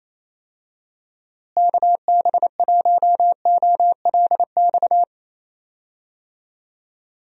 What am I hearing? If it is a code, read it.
Morse code, heard as a keyed tone, about 27 wpm, 700 Hz. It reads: KB1OLX
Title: USA Call Signs - 6 Characters - Repeated 2x - 28wpm